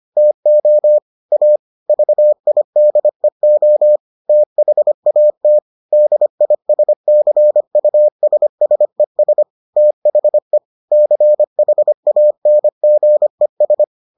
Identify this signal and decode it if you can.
Morse code, heard as a keyed tone, about 25 wpm, 600 Hz. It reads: TO A VIDEO THAT DISCUSSES THE CHANGES